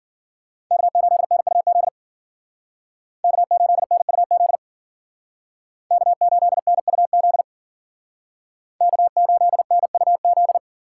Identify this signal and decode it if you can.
Morse code, heard as a keyed tone, about 51 wpm, 700 Hz. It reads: K8NUZ K8NUZ K8NUZ K8NUZ